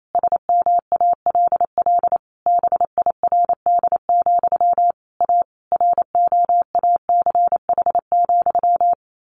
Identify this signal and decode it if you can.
Morse code, heard as a keyed tone, about 28 wpm, 700 Hz. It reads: SMALL BIRD, A ROACH,
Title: A Sound Of Thunder 28wpm